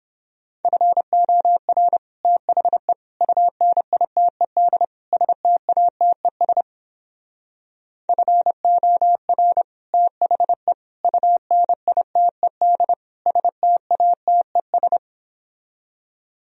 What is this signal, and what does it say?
Morse code, heard as a keyed tone, about 29 wpm, 700 Hz. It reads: FOR THE UNITED STATES FOR THE UNITED STATES